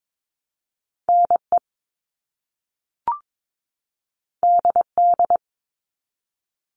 Morse code, heard as a keyed tone, about 22 wpm, 700 Hz. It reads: NE DD
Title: All Cut Numbers - 2-Digit 22wpm